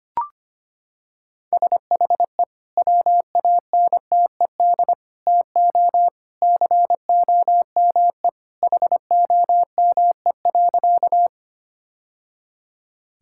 Morse code, heard as a keyed tone, about 25 wpm, 700 Hz. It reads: SHE WANTED TO COME HOME.
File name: practice / 25wpm / Sentences with 2-Word Phrases - 25wpm